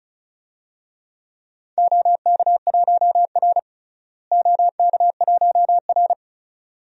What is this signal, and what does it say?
Morse code, heard as a keyed tone, about 34 wpm, 700 Hz. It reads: OK1R OK1R
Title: DXCC-100 Call Signs - 4 Characters - Repeated 2x - 35wpm